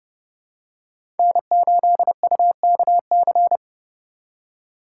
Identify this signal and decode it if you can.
Morse code, heard as a keyed tone, about 30 wpm, 700 Hz. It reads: N8UKC
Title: USA Call Signs - 4x Character Spacing - 30wpm